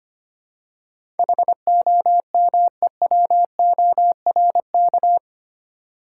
Morse code, heard as a keyed tone, about 25 wpm, 700 Hz. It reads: HOMEWORK
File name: Eight Letter Words - 25wpm